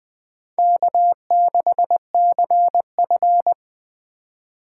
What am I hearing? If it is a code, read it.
Morse code, heard as a keyed tone, about 20 wpm, 700 Hz. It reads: K6CF
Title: Speed Racing - USA 1x2 Call Signs 17wpm